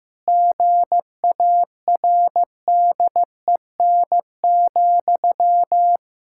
Morse code, heard as a keyed tone, about 16 wpm, 700 Hz. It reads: GARDEN,
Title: Pangram Challenge v2 - 15wpm